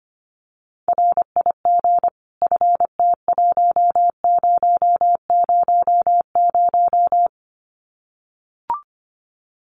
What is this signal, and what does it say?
Morse code, heard as a keyed tone, about 25 wpm, 700 Hz. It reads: RIG FT1000
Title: Common QSO Phrases 25wpm